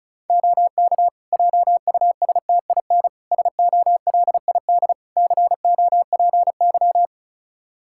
Morse code, heard as a keyed tone, about 34 wpm, 700 Hz. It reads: OK JUSTIN SOLID COPY
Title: Practice QSO - Exchange 3 - More Info - 35wpm